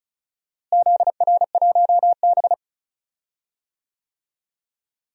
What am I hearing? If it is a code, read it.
Morse code, heard as a keyed tone, about 34 wpm, 700 Hz. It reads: ZR1B
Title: DXCC-100 Call Signs - 4 Characters - Repeated 2x - 35wpm